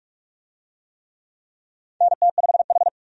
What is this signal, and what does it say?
Morse code, heard as a keyed tone, about 42 wpm, 700 Hz. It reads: NT5H
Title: USA Call Signs - 4 Characters - 45wpm